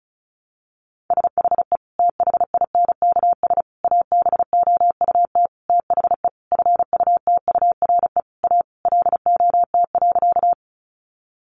Morse code, heard as a keyed tone, about 34 wpm, 700 Hz. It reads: SHE THINKS ABOUT THE FUTURE A LOT.
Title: Easy Phrases 35wpm